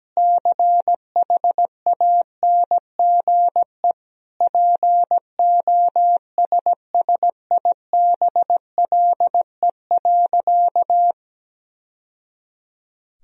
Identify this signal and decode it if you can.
Morse code, heard as a keyed tone, about 17 wpm, 700 Hz. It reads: CHANGE POSSIBLE.
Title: Sentences from Top 1000 Words - Review Part 1 - 17wpm